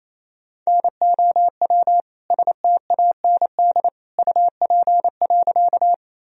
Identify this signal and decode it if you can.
Morse code, heard as a keyed tone, about 28 wpm, 700 Hz. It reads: NOW STAND UP.
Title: Sentences from Top 1000 Words - Review Part 1 - 28wpm